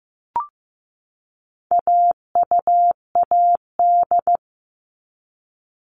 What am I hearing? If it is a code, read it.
Morse code, heard as a keyed tone, about 15 wpm, 700 Hz. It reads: AUAD
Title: All Cut Numbers - 4-Digit 15wpm